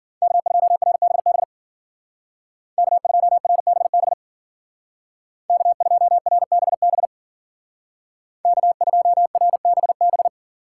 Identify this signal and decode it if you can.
Morse code, heard as a keyed tone, about 49 wpm, 700 Hz. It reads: K2RBB K2RBB K2RBB K2RBB
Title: Speed Racing - USA 1x3 Call Signs 40wpm